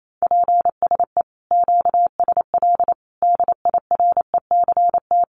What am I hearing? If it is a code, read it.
Morse code, heard as a keyed tone, about 28 wpm, 700 Hz. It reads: PSE QSL DIRECT
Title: Common QSO Phrases 28wpm